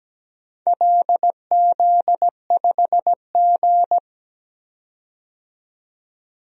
Morse code, heard as a keyed tone, about 17 wpm, 700 Hz. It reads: LZ5G
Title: DXCC-100 Call Signs - 4 Characters - 17wpm